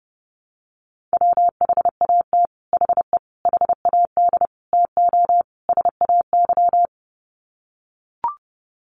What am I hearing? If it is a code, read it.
Morse code, heard as a keyed tone, about 30 wpm, 700 Hz. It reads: WHAT HE HAD TO SAY
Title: Sets of 5 Words Encore2 30wpm